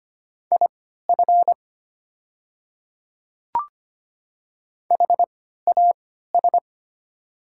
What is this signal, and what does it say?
Morse code, heard as a keyed tone, about 25 wpm, spaced as for 14 wpm, 700 Hz. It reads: IF HAS
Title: 3x Character Spacing - Top 100 Words - 25wpm